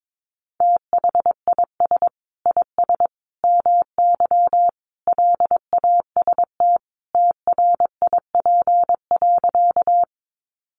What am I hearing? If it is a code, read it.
Morse code, heard as a keyed tone, about 22 wpm, 700 Hz. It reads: THIS IS MY LAST TRIP.
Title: Sentences from Top 1300 Words 22wpm